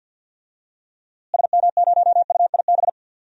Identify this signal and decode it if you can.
Morse code, heard as a keyed tone, about 47 wpm, 700 Hz. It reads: SM0UIB
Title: DXCC-100 Call Signs - 6 Characters - Repeated 3x - 50wpm